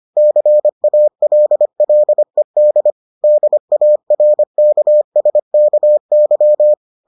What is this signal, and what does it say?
Morse code, heard as a keyed tone, about 25 wpm, 600 Hz. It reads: CALLED DARKSKY